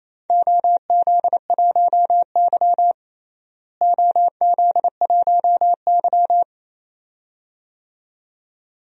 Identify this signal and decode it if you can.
Morse code, heard as a keyed tone, about 28 wpm, 700 Hz. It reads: OZ1Y OZ1Y
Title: DXCC-100 Call Signs - 4 Characters - Repeated 2x - 28wpm